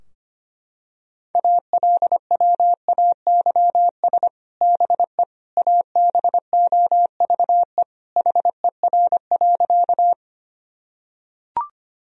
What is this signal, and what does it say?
Morse code, heard as a keyed tone, about 25 wpm, 700 Hz. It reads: ALWAYS BE ABOVE HER.
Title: Sentences from Top 1000 Words - Review Part 2 - 25wpm